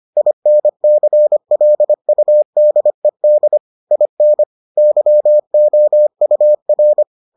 Morse code, heard as a keyed tone, about 25 wpm, 600 Hz. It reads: INCLUDED IN YOUR